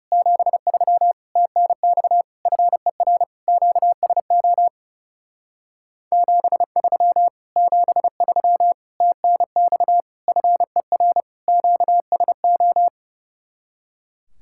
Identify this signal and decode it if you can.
Morse code, heard as a keyed tone, about 38 wpm, 700 Hz. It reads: 73 TNX FER QSO 73 73 TNX FER QSO